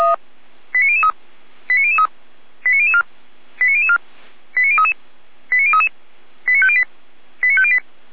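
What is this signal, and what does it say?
Five-tone selective call (ZVEI), heard as quick runs of five tones, tones of about 80 ms. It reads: 89003 89003 89004 89004 89033 89033 89508 89508
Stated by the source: ZVEI five-tone selective call received by SDR, Feuerwehr Orlamünde.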